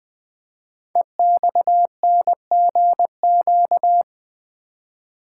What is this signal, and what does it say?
Morse code, heard as a keyed tone, about 20 wpm, 700 Hz. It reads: EXNGQ
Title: Code Groups - 5 Letters - 20wpm